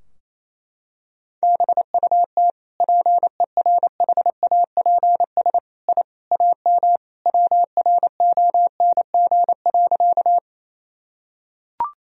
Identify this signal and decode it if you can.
Morse code, heard as a keyed tone, about 28 wpm, 700 Hz. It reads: BUT PERHAPS I AM WRONG.
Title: Sentences from Top 1000 Words - Review Part 2 - 28wpm